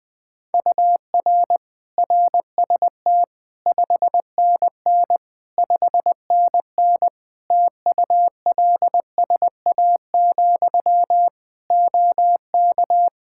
Morse code, heard as a keyed tone, about 20 wpm, 700 Hz. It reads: UR RST 5NN 5NN TULSA, OK
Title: Practice QSO - Exchange 1 - Beginning - 20wpm